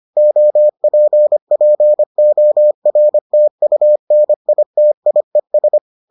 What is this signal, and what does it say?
Morse code, heard as a keyed tone, about 25 wpm, 600 Hz. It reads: OPPORTUNITIES